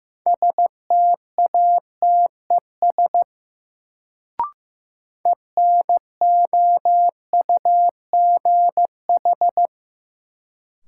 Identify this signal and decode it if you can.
Morse code, heard as a keyed tone, about 15 wpm, 700 Hz. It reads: STATES ENOUGH
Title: Sets of 2 Words - New Word Drill - 15wpm